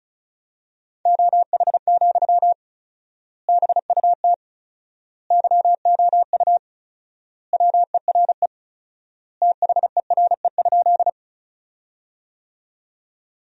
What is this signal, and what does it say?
Morse code, heard as a keyed tone, about 34 wpm, 700 Hz. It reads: OH, BUT YOU WERE THERE?
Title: 4x Word-Spacing - Sentences from Top 500 Words - 35wpm